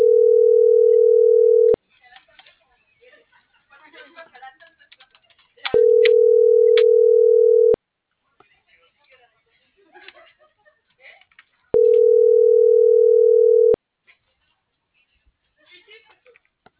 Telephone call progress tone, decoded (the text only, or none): ringback tone